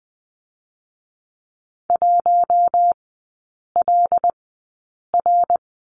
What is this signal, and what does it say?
Morse code, heard as a keyed tone, about 20 wpm, spaced as for 8 wpm, 700 Hz. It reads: 1LR
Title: 121 Lesson 20e - Number 3 - Three Characters 20wpm